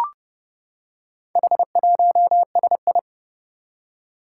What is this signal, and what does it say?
Morse code, heard as a keyed tone, about 28 wpm, 700 Hz. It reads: H1SI
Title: Four Challenging Letter-Number Combinations 30wpm